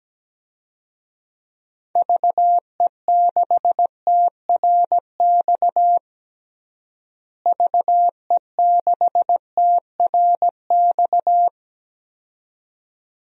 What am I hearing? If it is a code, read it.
Morse code, heard as a keyed tone, about 17 wpm, 700 Hz. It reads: VE6TRX VE6TRX
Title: DXCC-100 Call Signs - 6 Characters - Repeated 2x - 17wpm